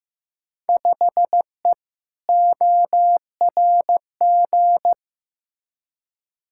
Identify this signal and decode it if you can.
Morse code, heard as a keyed tone, about 15 wpm, 700 Hz. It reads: 5E ORG
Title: Field Day Exchanges Encore - 15wpm